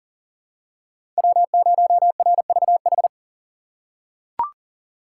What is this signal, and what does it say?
Morse code, heard as a keyed tone, about 40 wpm, 700 Hz. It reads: W0RVH
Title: USA Call Signs - 5 Characters - 4x Character Spacing - 40wpm